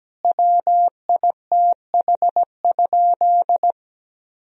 Morse code, heard as a keyed tone, about 17 wpm, 700 Hz. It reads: WITH?
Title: Sentences from Top 1000 Words - Review Part 4 - 17wpm